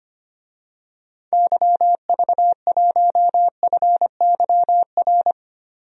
Morse code, heard as a keyed tone, about 25 wpm, 700 Hz. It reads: YV1FYR
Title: DXCC-100 Call Signs - 6 Characters - 25wpm